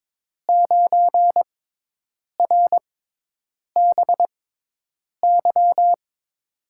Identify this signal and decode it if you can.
Morse code, heard as a keyed tone, about 22 wpm, spaced as for 8 wpm, 700 Hz. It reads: 9RBY
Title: USA Call Signs - 6 Characters - 6x Character Spacing - 22wpm